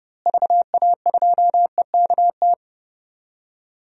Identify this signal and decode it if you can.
Morse code, heard as a keyed tone, about 32 wpm, 700 Hz. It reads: VA2EKT
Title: DXCC-100 Call Signs - 6 Characters - Repeated 2x - 30wpm